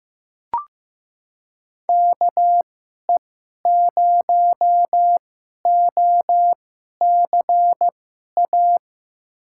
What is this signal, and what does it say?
Morse code, heard as a keyed tone, about 15 wpm, spaced as for 11 wpm, 700 Hz. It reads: KE0OCA
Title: USA Call Signs - 6 Characters - 2x Character Spacing - 15wpm